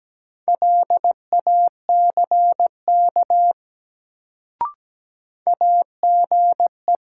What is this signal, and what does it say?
Morse code, heard as a keyed tone, about 17 wpm, 700 Hz. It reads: LACK AGE
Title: Sets of 3 Words - New Word Drill - 17wpm